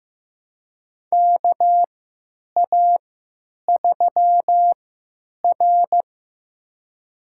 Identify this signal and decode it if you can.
Morse code, heard as a keyed tone, about 15 wpm, spaced as for 9 wpm, 700 Hz. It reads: KA3R
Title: USA Call Signs - 4 Characters - 3x Character Spacing - 15wpm